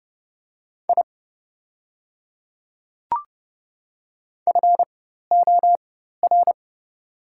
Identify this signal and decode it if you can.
Morse code, heard as a keyed tone, about 30 wpm, spaced as for 14 wpm, 700 Hz. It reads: I FOR
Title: 4x Character Spacing - Top 100 Words - 30wpm